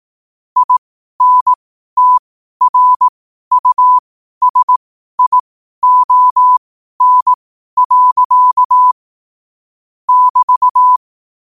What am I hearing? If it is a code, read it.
Morse code, heard as a keyed tone, about 18 wpm, spaced as for 12 wpm, 1000 Hz. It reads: INTRUSION. <BT>
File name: qotdfiles / 12wpm